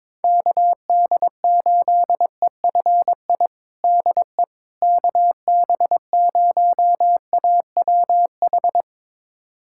Morse code, heard as a keyed tone, about 22 wpm, 700 Hz. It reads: KD8EFI DE KB0AWH
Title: USA Callsign DE Callsign - 6 Characters - 22wpm